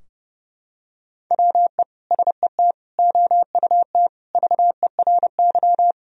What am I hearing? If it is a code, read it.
Morse code, heard as a keyed tone, about 30 wpm, 700 Hz. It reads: WE SET OUT VERY
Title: Sentences from Top 1000 Words - Review Part 2 - 30wpm